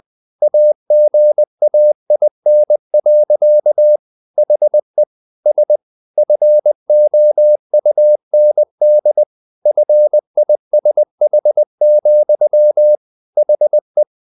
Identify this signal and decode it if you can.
Morse code, heard as a keyed tone, about 20 wpm, 600 Hz. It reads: AGAIN. HE S FOUND FISH, HE
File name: the old man and the sea / the old man and the sea part 1 20wpm